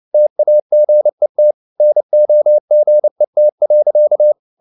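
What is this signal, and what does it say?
Morse code, heard as a keyed tone, about 29 wpm, 600 Hz. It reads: TAGET NOGET.
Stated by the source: Fast (QRQ) Morse at 29 wpm, from CW med Gnister.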